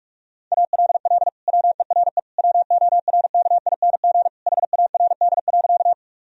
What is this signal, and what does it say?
Morse code, heard as a keyed tone, about 41 wpm, 700 Hz. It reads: ALL WERE WORKING HARD.
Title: Sentences from Top 1000 Words - Review Part 4 - 45wpm